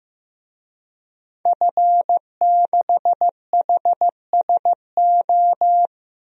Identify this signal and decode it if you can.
Morse code, heard as a keyed tone, about 15 wpm, 700 Hz. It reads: F6HSO
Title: DXCC-100 Call Signs - 5 Characters - Repeated 3x - 15wpm